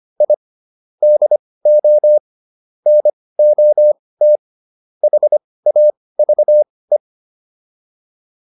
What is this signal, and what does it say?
Morse code, heard as a keyed tone, about 25 wpm, spaced as for 18 wpm, 600 Hz. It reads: I DO NOT HAVE